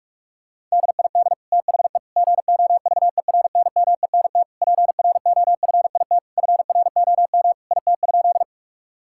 Morse code, heard as a keyed tone, about 46 wpm, 700 Hz. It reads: DID THE GOVERNMENT PROFIT FROM IT?
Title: Sentences from Top 1000 Words - Review Part 5 - 45wpm